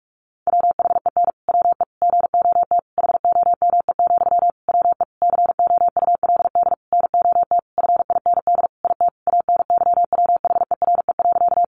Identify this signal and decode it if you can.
Morse code, heard as a keyed tone, about 44 wpm, 700 Hz. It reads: WHEN WE GOT HOME, WE COULD NOT FIND IT ANYWHERE.
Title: Sentences from Top 1400 Words 45wpm